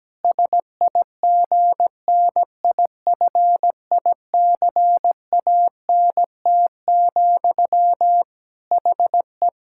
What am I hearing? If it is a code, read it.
Morse code, heard as a keyed tone, about 17 wpm, 700 Hz. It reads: SIGNIFICANT, HE
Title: Sentences from Top 1000 Words - Review Part 6 - 17wpm